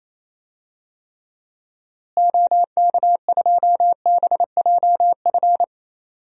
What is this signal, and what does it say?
Morse code, heard as a keyed tone, about 28 wpm, 700 Hz. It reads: OK2BJF
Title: DXCC-100 Call Signs - 6 Characters - Repeated 3x - 28wpm